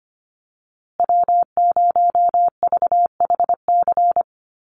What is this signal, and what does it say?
Morse code, heard as a keyed tone, about 25 wpm, 700 Hz. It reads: W0VHC
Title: USA Call Signs Repeated 3x 25wpm